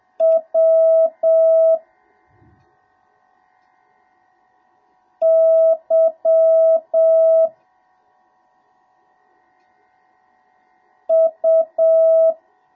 Morse code, heard as a keyed tone, about 7 wpm, 1300 Hz, with noise. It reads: W Y U